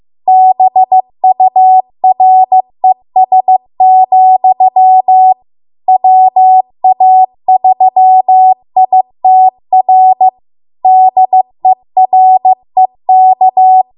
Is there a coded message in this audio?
Yes: Morse code from 0.3 to 13.8 s, heard as a keyed tone, about 15 wpm, 750 Hz. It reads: BURES, WA3ITR DEREK